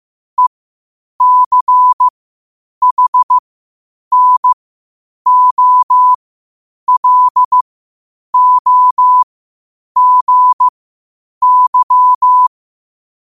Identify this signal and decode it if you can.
Morse code, heard as a keyed tone, about 15 wpm, spaced as for 8 wpm, 1000 Hz. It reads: ECHNOLOGY
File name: qotdfiles / 8wpm